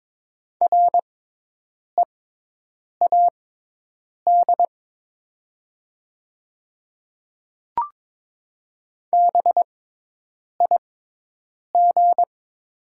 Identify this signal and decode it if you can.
Morse code, heard as a keyed tone, about 22 wpm, spaced as for 8 wpm, 700 Hz. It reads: READ BIG
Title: 6x Character Spacing - Top 200 Words - 22wpm